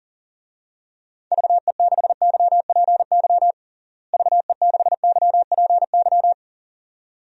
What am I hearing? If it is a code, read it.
Morse code, heard as a keyed tone, about 40 wpm, 700 Hz. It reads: VE6YPY VE6YPY
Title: DXCC-100 Call Signs - 6 Characters - Repeated 2x - 40wpm